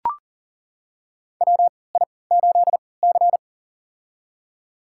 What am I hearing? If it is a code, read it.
Morse code, heard as a keyed tone, about 40 wpm, spaced as for 23 wpm, 700 Hz. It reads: WI8C
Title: USA Call Signs - 4 Characters - 3x Character Spacing - 40wpm